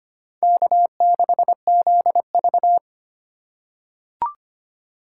Morse code, heard as a keyed tone, about 25 wpm, 700 Hz. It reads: K6ZV
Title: USA Call Signs - 4 Characters - 3x Character Spacing - 25wpm